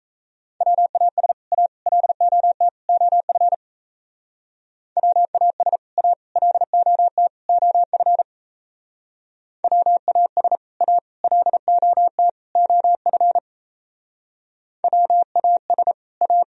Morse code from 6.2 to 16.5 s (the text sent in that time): LOT OF WAS A LOT OF WAS A